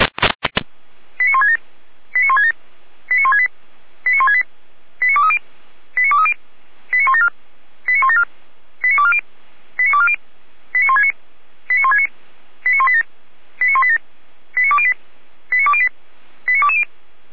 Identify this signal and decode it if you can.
Five-tone selective call (ZVEI), heard as quick runs of five tones, tones of about 70 ms. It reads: 89167 89167 89168 89168 89230 89230 89174 89174 89240 89240 89169 89169 89187 89187 89208 89208 89220